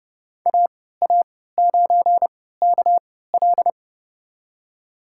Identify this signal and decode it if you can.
Morse code, heard as a keyed tone, about 30 wpm, spaced as for 17 wpm, 700 Hz. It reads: AA9KL
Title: USA Call Signs - 5 Characters - 3x Character Spacing - 30wpm